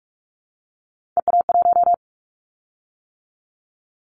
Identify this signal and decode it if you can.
Morse code, heard as a keyed tone, about 44 wpm, 700 Hz. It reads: EA1
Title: Callsign Prefixes for Top 25 DXCC - 45wpm